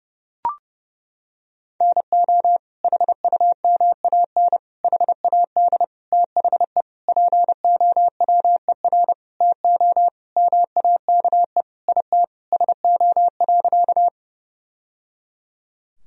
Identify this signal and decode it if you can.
Morse code, heard as a keyed tone, about 30 wpm, 700 Hz. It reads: NO HUMAN HAD THE POWER TO MAKE IT SO.
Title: Sentences from Top 1000 Words - Review Part 3 - 30wpm